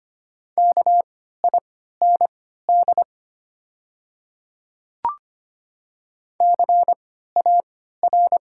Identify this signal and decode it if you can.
Morse code, heard as a keyed tone, about 25 wpm, spaced as for 14 wpm, 700 Hz. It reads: KIND CAR
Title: 3x Character Spacing - Top 300 Words - 25wpm